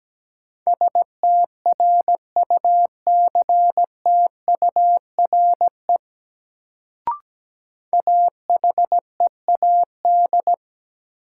Morse code, heard as a keyed tone, about 17 wpm, 700 Hz. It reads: STRUCTURE AHEAD
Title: Sentences from Top 1000 Words - Review Part 6 - New Word Drill - 17wpm